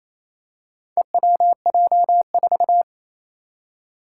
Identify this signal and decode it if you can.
Morse code, heard as a keyed tone, about 28 wpm, 700 Hz. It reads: EWJ4
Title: Four Challenging Letter-Number Combinations 28wpm